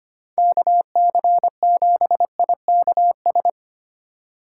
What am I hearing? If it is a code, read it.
Morse code, heard as a keyed tone, about 24 wpm, 700 Hz. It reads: KC7IKS
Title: Speed Racing - USA 2x3 Call Signs 25wpm